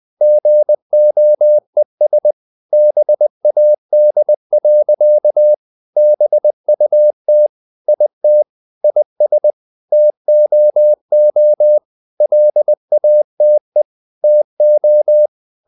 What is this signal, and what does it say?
Morse code, heard as a keyed tone, about 20 wpm, 600 Hz. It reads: GOES BAD. BUT IT IS TOO LATE TO
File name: the old man and the sea part 2 20wpm